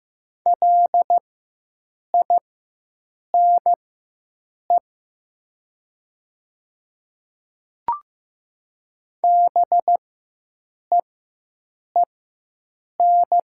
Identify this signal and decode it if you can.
Morse code, heard as a keyed tone, about 15 wpm, spaced as for 7 wpm, 700 Hz. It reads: LINE BEEN
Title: 4x Character Spacing - Top 200 Words - 15wpm